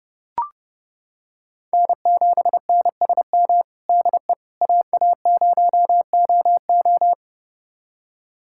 Morse code, heard as a keyed tone, about 30 wpm, 700 Hz. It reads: N7NSM DE AA0OO
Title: USA Callsign DE Callsign - 5 Characters - 30wpm